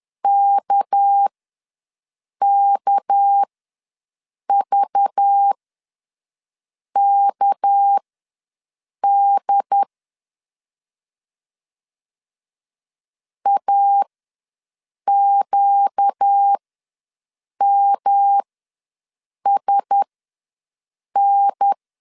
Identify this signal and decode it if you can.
Morse code, heard as a keyed tone, about 10 wpm, spaced as for 6 wpm, 800 Hz. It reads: KKVKD AQMSN
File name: LEZIONI / 6 PKX0 / 6 PK ws 10wpm 3745